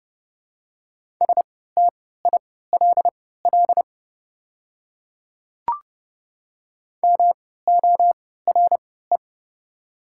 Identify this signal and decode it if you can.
Morse code, heard as a keyed tone, about 32 wpm, spaced as for 17 wpm, 690 Hz. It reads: STILL MORE
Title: 3x Character Spacing - Top 300 Words - 30wpm